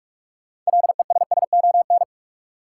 Morse code, heard as a keyed tone, about 44 wpm, 700 Hz. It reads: LESSON
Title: Sentences from Top 1200 Words - New Word Drill Alt - 45wpm